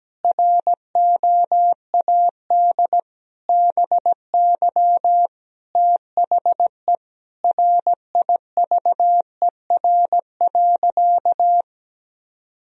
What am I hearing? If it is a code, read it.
Morse code, heard as a keyed tone, about 17 wpm, 700 Hz. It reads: ROAD BY THE RIVER.
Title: Sentences from Top 1000 Words - Review Part 4 - 17wpm